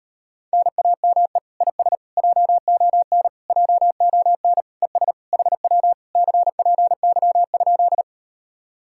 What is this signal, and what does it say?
Morse code, heard as a keyed tone, about 38 wpm, 700 Hz. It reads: NAME IS JON JON ES HW CPY?